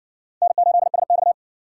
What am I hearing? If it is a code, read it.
Morse code, heard as a keyed tone, about 54 wpm, 700 Hz. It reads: N8SX